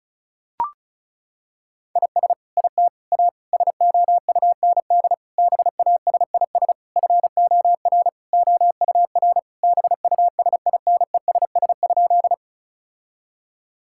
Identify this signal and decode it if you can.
Morse code, heard as a keyed tone, about 36 wpm, 700 Hz. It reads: IS IT A SOUND BASIS FOR OUR BUSINESS?
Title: Sentences from Top 1000 Words - Review Part 5 - 35wpm